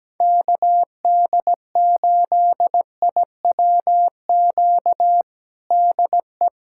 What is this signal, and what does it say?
Morse code, heard as a keyed tone, about 17 wpm, 700 Hz. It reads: KD8IWQ DE
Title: USA Callsign DE Callsign - 6 Characters - 17wpm